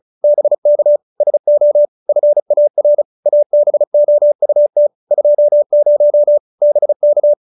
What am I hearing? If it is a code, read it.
Morse code, heard as a keyed tone, about 35 wpm, 570 Hz. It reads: BK SO FAR ABOUT 20 BK
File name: QSO312 35